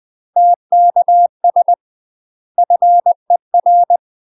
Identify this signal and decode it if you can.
Morse code, heard as a keyed tone, about 20 wpm, 700 Hz. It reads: TKS FER
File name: UK MASTER QSOs TRAINING / 20WPM / UK MASTER QSOs 0043